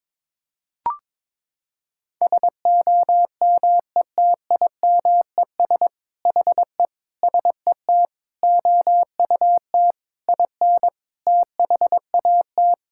Morse code, heard as a keyed tone, about 22 wpm, 700 Hz. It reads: SOMETIMES HE SET OUT IN THAT
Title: Sentences from Top 1000 Words - Review Part 5 - 22wpm